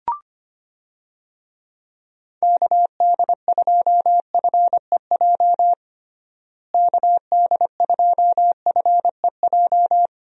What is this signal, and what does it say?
Morse code, heard as a keyed tone, about 25 wpm, 700 Hz. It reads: KD2FEJ KD2FEJ
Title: USA Call Signs - 6 Characters - Repeated 2x - 25wpm